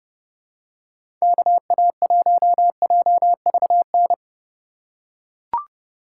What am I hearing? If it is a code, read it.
Morse code, heard as a keyed tone, about 30 wpm, 700 Hz. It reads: KA1JVN
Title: USA Call Signs - 3x Character Spacing - 30wpm